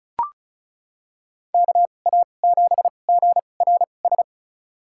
Morse code, heard as a keyed tone, about 34 wpm, spaced as for 25 wpm, 700 Hz. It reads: KA7GRS
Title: USA Call Signs - 6 Characters - 2x Character Spacing - 35wpm